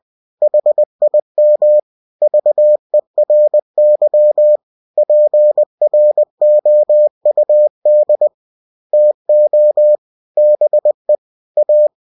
Morse code, heard as a keyed tone, about 20 wpm, 600 Hz. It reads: HIM VERY PROUD TO BE A
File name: cw-books / winnie-the-pooh / winnie-20wpm-page-0009